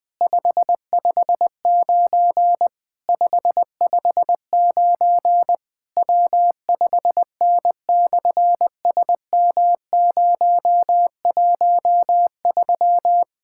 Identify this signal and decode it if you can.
Morse code, heard as a keyed tone, about 20 wpm, 700 Hz. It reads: 559 559 W5N/SM013
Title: SOTA QSO Practice 20wpm